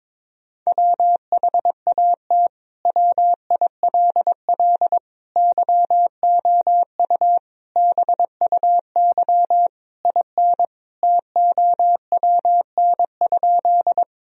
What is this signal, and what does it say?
Morse code, heard as a keyed tone, about 21 wpm, 700 Hz. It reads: WHAT WILL YOU BUY IN TOWN?
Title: Sentences from Top 1000 Words - Review Part 4 - 22wpm